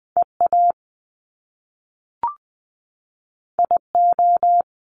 Morse code, heard as a keyed tone, about 20 wpm, 700 Hz. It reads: E A I O
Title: Top Two Letter Combinations in Words 20wpm